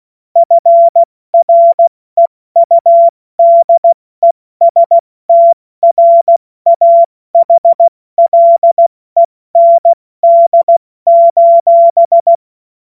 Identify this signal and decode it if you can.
Morse code, heard as a keyed tone, about 16 wpm, 680 Hz. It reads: FREUDESTRAHLEND:
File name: Morsekurs / Witze / Witz 21. Dezember 2021, 70 F80